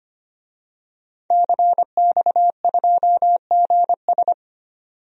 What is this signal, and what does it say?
Morse code, heard as a keyed tone, about 25 wpm, 700 Hz. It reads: CX2GS
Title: DXCC-100 Call Signs - 5 Characters - 25wpm